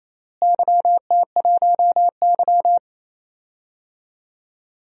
Morse code, heard as a keyed tone, about 28 wpm, 700 Hz. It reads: YT1Y
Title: DXCC-100 Call Signs - 4 Characters - Repeated 2x - 28wpm